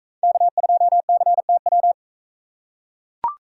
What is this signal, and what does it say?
Morse code, heard as a keyed tone, about 40 wpm, 700 Hz. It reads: K2CTW